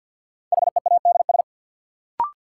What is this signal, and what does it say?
Morse code, heard as a keyed tone, about 46 wpm, 700 Hz. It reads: HEADS
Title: Sentences from Top 1000 Words - Review Part 3 - New Word Drill - 50wpm